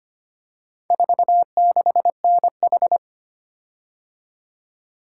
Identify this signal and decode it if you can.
Morse code, heard as a keyed tone, about 25 wpm, 700 Hz. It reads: 46NH
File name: practice / 25wpm / Four Letter-Number Combinations - 25wpm